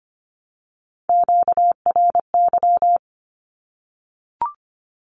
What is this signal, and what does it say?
Morse code, heard as a keyed tone, about 25 wpm, 700 Hz. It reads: QRY
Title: Q-Codes 25wpm